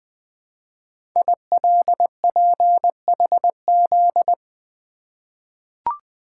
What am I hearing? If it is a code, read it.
Morse code, heard as a keyed tone, about 20 wpm, 700 Hz. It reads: ILPHZ
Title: Code Groups - 5 Letter:Numbers - 20wpm